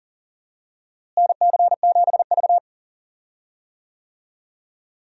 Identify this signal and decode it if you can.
Morse code, heard as a keyed tone, about 40 wpm, 700 Hz. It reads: NC7V
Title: USA Call Signs - 4 Characters - Repeated 2x - 40wpm